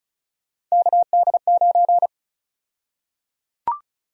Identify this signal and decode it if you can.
Morse code, heard as a keyed tone, about 34 wpm, 700 Hz. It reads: KD9
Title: USA Call Sign Prefixes - 35wpm